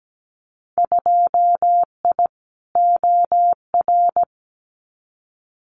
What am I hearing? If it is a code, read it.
Morse code, heard as a keyed tone, about 17 wpm, 700 Hz. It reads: 2I OR
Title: Winter Field Day Exchanges Encore 17wpm